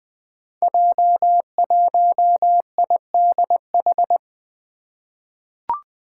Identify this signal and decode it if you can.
Morse code, heard as a keyed tone, about 20 wpm, 700 Hz. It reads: J1IDH
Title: Five Challenging Letter-Number Combinations 20wpm